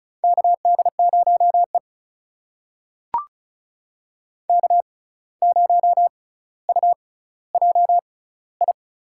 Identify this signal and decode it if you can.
Morse code, heard as a keyed tone, about 36 wpm, spaced as for 17 wpm, 700 Hz. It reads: KD0E K0UJI
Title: USA Call Signs - 6x Character Spacing - 35wpm